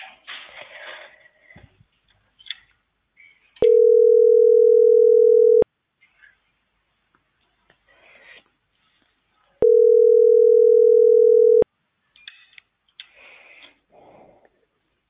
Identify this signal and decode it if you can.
Ringback tone (North America), 440 + 480 Hz.